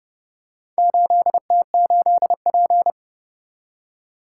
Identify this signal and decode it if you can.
Morse code, heard as a keyed tone, about 30 wpm, 700 Hz. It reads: 8T8P